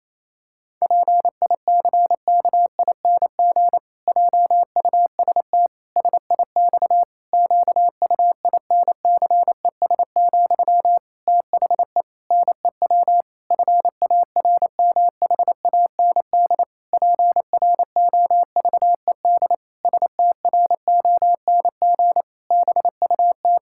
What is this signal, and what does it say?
Morse code, heard as a keyed tone, about 28 wpm, 700 Hz. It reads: PICKING JUST SIX QUINCES, THE NEW FARMHAND PROVED STRONG BUT
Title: Pangram Challenge v2 - 28wpm